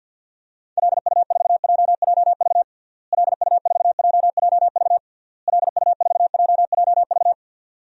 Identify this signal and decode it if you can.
Morse code, heard as a keyed tone, about 52 wpm, 700 Hz. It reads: LU4JJV LU4JJV LU4JJV
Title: DXCC-100 Call Signs - 6 Characters - Repeated 3x - 50wpm